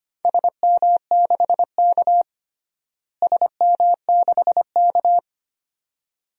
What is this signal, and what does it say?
Morse code, heard as a keyed tone, about 25 wpm, 700 Hz. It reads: SM6K SM6K
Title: DXCC-100 Call Signs - 4 Characters - Repeated 2x - 25wpm